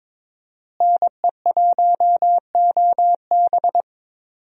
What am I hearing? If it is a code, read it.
Morse code, heard as a keyed tone, about 22 wpm, 700 Hz. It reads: NE1OB
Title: USA Call Signs - 5 Characters - Repeated 2x - 22wpm